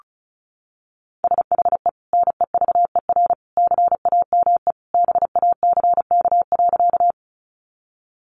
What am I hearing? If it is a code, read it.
Morse code, heard as a keyed tone, about 34 wpm, 700 Hz. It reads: SHE NEVER CAME BACK.
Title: Sentences from Top 300 Words 35wpm